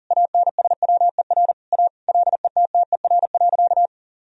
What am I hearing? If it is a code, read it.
Morse code, heard as a keyed tone, about 40 wpm, 700 Hz. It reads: ANSWER A LETTER.
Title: Sentences from Top 1000 Words - Review Part 1 - 40wpm